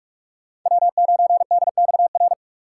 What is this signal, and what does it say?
Morse code, heard as a keyed tone, about 46 wpm, 700 Hz. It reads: W9DXR
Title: K1USN SST QSO Practice - 45wpm